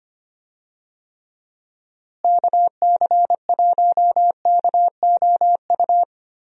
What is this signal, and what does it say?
Morse code, heard as a keyed tone, about 25 wpm, 700 Hz. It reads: KC1KOU
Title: USA Call Signs - 6 Characters - Repeated 3x - 25wpm